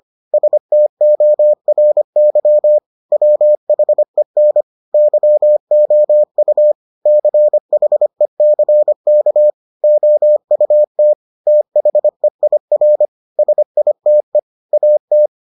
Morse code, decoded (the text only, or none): STORY WHEN YOU CHECK OUT THEIR SITE AT